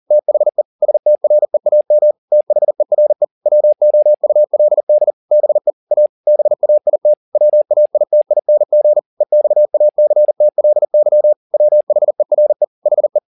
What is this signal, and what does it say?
Morse code, heard as a keyed tone, about 40 wpm, 600 Hz. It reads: THE STREAM THERE WOULD BE A BAIT WAITING EXACTLY WHERE HE